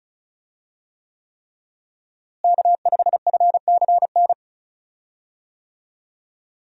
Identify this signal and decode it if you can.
Morse code, heard as a keyed tone, about 36 wpm, 700 Hz. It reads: K5FCN
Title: USA Call Signs - 5 Characters - Repeated 3x - 35wpm